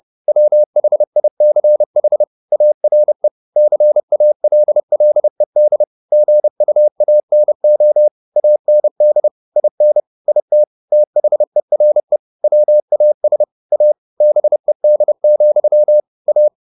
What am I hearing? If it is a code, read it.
Morse code, heard as a keyed tone, about 30 wpm, 600 Hz. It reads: WHICH ARE CALLED GUANO AND IN IT THERE WAS A BED, A
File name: the old man and the sea / the old man and the sea 1 30wpm